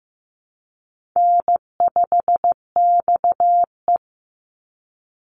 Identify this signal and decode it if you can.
Morse code, heard as a keyed tone, about 15 wpm, 700 Hz. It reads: N5XE
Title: Easy Call Signs - 15wpm